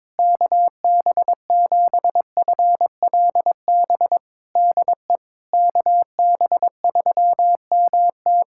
Morse code, heard as a keyed tone, about 22 wpm, 700 Hz. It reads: KB7FLB DE KB3MT
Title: Practice QSO - Exchange 4 - Ending - 22wpm